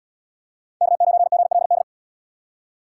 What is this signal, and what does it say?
Morse code, heard as a keyed tone, about 71 wpm, 700 Hz. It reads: K9GLD